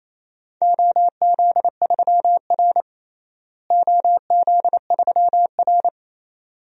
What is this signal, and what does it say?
Morse code, heard as a keyed tone, about 27 wpm, 700 Hz. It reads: OZ3R OZ3R
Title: DXCC-100 Call Signs - 4 Characters - Repeated 2x - 28wpm